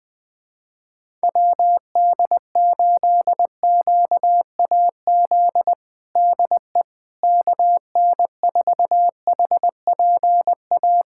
Morse code, heard as a keyed tone, about 20 wpm, 700 Hz. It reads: WD8QAZ DE KN4HPA
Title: USA Callsign DE Callsign - 6 Characters - 20wpm